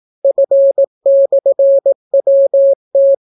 Morse code, heard as a keyed tone, about 18 wpm, 550 Hz. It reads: F/WT